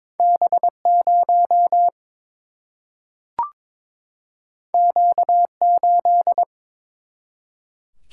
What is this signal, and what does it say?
Morse code, heard as a keyed tone, about 22 wpm, 700 Hz. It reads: B0 Q8
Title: Single-Letter Single-Number 22wpm